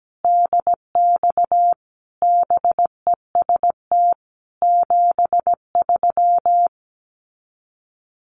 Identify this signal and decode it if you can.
Morse code, heard as a keyed tone, about 17 wpm, 700 Hz. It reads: DX BEST 73
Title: Common QSO Phrases 17wpm